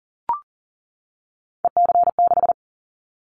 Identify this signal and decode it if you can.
Morse code, heard as a keyed tone, about 40 wpm, 700 Hz. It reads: EC6
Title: Callsign Prefixes for Top 100 DXCC - 40wpm